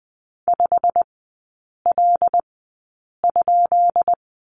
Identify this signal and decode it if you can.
Morse code, heard as a keyed tone, about 20 wpm, spaced as for 8 wpm, 690 Hz. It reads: 5L?
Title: 133 Lesson 22e - Question Mark - Three Characters 20wpm